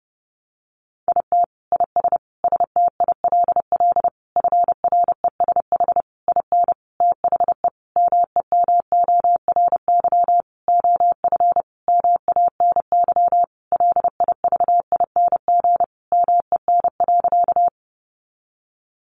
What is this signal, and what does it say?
Morse code, heard as a keyed tone, about 28 wpm, 700 Hz. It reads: IT IS STILL FRESH IN THE MEMORY OF MANY LIVING MEN.
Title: Sentences from Top 1300 Words 30wpm